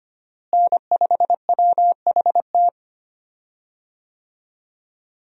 Morse code, heard as a keyed tone, about 25 wpm, 700 Hz. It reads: N5WHT
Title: USA Call Signs - 5 Characters - Repeated 2x - 25wpm